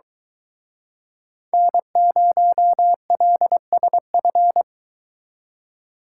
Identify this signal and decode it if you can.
Morse code, heard as a keyed tone, about 23 wpm, 700 Hz. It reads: N0LSF